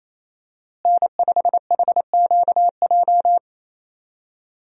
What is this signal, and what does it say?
Morse code, heard as a keyed tone, about 28 wpm, 700 Hz. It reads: N5HQJ
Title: USA Call Signs - 5 Characters - Repeated 3x - 28wpm